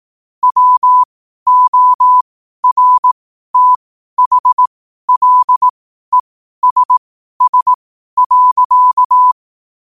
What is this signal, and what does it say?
Morse code, heard as a keyed tone, about 18 wpm, spaced as for 12 wpm, 1000 Hz. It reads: WORTHLESS.
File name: qotdfiles / 12wpm